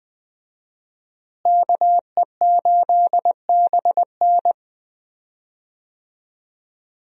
Morse code, heard as a keyed tone, about 20 wpm, 700 Hz. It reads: KE8BN
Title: USA Call Signs - 5 Characters - Repeated 3x - 20wpm